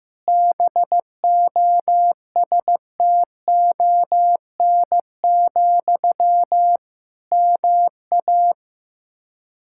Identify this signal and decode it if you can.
Morse code, heard as a keyed tone, about 15 wpm, 700 Hz. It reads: BOSTON, MA
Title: K1USN SST QSO Practice 15wpm